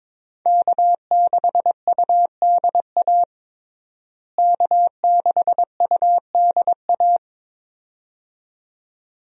Morse code, heard as a keyed tone, about 22 wpm, 700 Hz. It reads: K6UDA K6UDA
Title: USA Call Signs - 5 Characters - Repeated 2x - 22wpm